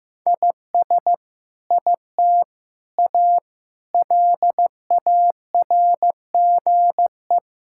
Morse code, heard as a keyed tone, about 15 wpm, 700 Hz. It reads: IS IT A LARGE
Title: Sentences from Top 1000 Words - Review Part 6 - 15wpm